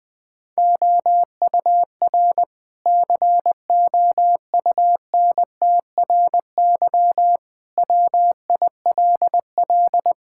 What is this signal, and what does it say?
Morse code, heard as a keyed tone, about 20 wpm, 700 Hz. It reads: OUR COUNTRY WILL
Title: Sentences from Top 1000 Words - Review Part 6 - 20wpm